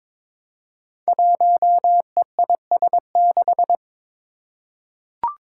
Morse code, heard as a keyed tone, about 22 wpm, 700 Hz. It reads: 1EIS6
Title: Five Challenging Letter-Number Combinations 22wpm